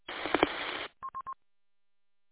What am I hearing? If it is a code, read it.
There is no code.